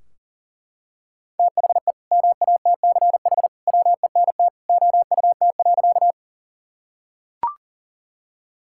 Morse code, heard as a keyed tone, about 40 wpm, 700 Hz. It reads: THE MATCH WENT OUT.
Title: Sentences from Top 1600 Words 40wpm